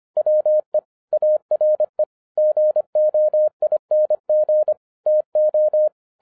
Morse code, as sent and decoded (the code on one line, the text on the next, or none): .-- . / .- .-. / --. --- .. -. --. / - ---
WE ARE GOING TO